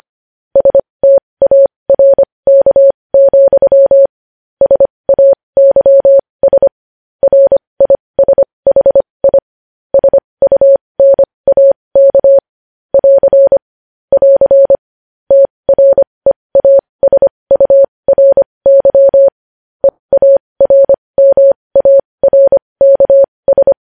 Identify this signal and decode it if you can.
Morse code, heard as a keyed tone, about 25 wpm, spaced as for 20 wpm, 550 Hz. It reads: STARK, SAYS RISHI SUNAK <AR> <AR> TREASURY EARMARKS
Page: news-headlines-in-morse-code-at-20-wpm